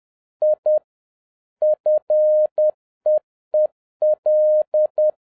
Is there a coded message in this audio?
Yes: Morse code from 0.4 to 5.1 s, heard as a keyed tone, about 10 wpm, 610 Hz. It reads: I FEEL